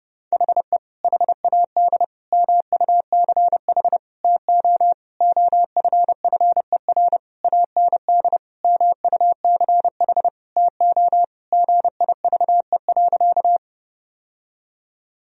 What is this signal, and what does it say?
Morse code, heard as a keyed tone, about 32 wpm, 700 Hz. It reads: HE HAD MUCH TO OFFER AND MUCH TO GIVE.
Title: Sentences from Top 1000 Words - Review Part 4 - 30wpm